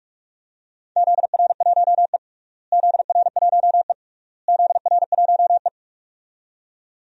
Morse code, heard as a keyed tone, about 42 wpm, 700 Hz. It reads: ZR1E ZR1E ZR1E
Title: DXCC-100 Call Signs - 4 Characters - Repeated 3x - 45wpm